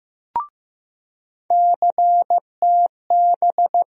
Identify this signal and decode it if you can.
Morse code, heard as a keyed tone, about 15 wpm, 700 Hz. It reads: CTB
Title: USA Call Sign Postfixes - 3 Letters 15wpm